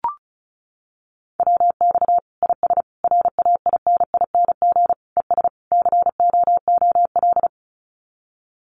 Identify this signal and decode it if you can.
Morse code, heard as a keyed tone, about 34 wpm, 700 Hz. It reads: WX IS RAINING ES COOL
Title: Common QSO Phrases 35wpm